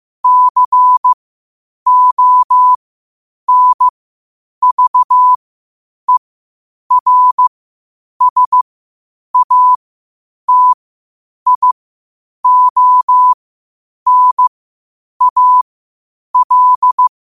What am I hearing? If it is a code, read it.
Morse code, heard as a keyed tone, about 15 wpm, spaced as for 8 wpm, 1000 Hz. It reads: CONVERSATIONAL